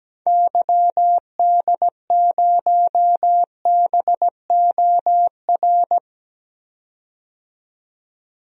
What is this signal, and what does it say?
Morse code, heard as a keyed tone, about 17 wpm, 700 Hz. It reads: YD0BOR
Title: DXCC-100 Call Signs - 6 Characters - 17wpm